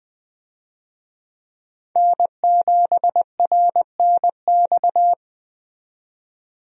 Morse code, heard as a keyed tone, about 20 wpm, 700 Hz. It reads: N7RNX
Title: USA Call Signs - 5 Characters - 20wpm